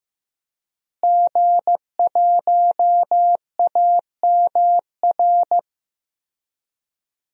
Morse code, heard as a keyed tone, about 15 wpm, 700 Hz. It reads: G1AMR
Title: DXCC-100 Call Signs - 5 Characters - Repeated 3x - 15wpm